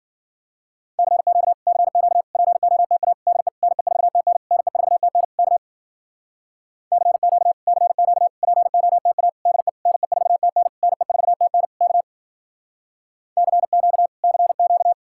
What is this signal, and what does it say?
Morse code, heard as a keyed tone, about 52 wpm, 700 Hz. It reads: CQ CQ POTA DE NE4TN NE4TN K CQ CQ POTA DE NE4TN NE4TN K CQ CQ